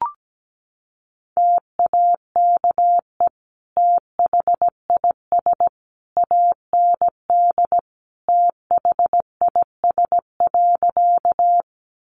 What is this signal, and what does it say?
Morse code, heard as a keyed tone, about 17 wpm, 700 Hz. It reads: TAKE THIS AND THIS.
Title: Sentences from Top 200 Words 17wpm